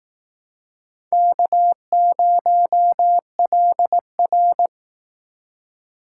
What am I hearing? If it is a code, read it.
Morse code, heard as a keyed tone, about 19 wpm, 700 Hz. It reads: K0LR
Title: Speed Racing - USA Call Signs 15wpm